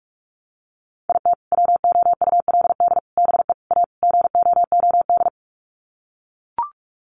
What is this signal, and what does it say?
Morse code, heard as a keyed tone, about 44 wpm, 700 Hz. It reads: IT WOULD BE A GOOD
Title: Sets of 5 Words Encore2 45wpm